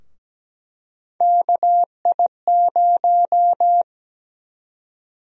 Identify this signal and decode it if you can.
Morse code, heard as a keyed tone, about 17 wpm, 700 Hz. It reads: KI0
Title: USA Call Sign Prefixes - 17wpm